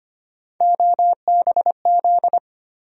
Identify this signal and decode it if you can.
Morse code, heard as a keyed tone, about 25 wpm, 700 Hz. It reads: OBZ
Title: USA Call Sign Postfixes - 3 Letters 25wpm